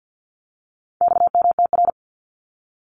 Morse code, heard as a keyed tone, about 50 wpm, 700 Hz. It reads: XMTR
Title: Abbreviations Repeated 3x - 50wpm